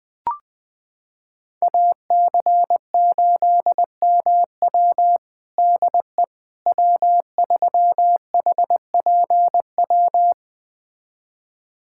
Morse code, heard as a keyed tone, about 20 wpm, 700 Hz. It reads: AC8MW DE W3HPW
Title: USA Callsign DE Callsign - 5 Characters - 20wpm